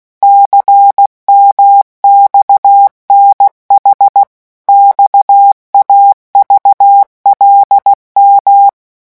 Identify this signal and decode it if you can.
Morse code, heard as a keyed tone, about 16 wpm, 800 Hz. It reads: CMXNH XAVLM
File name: morse mp3 / revision5-vitesse 15 mots